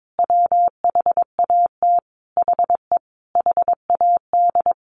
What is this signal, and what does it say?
Morse code, heard as a keyed tone, about 22 wpm, 700 Hz. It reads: WHAT HE HAD
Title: Sentences from Top 1100 Words 22wpm